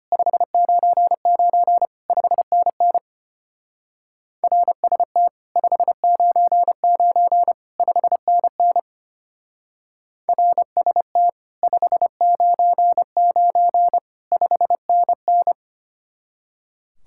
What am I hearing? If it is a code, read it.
Morse code, heard as a keyed tone, about 32 wpm, 700 Hz. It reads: 599 5NN RST 599 5NN RST 599 5NN